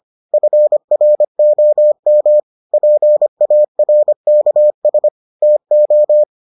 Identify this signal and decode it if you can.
Morse code, heard as a keyed tone, about 25 wpm, 600 Hz. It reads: FROM PARKS TO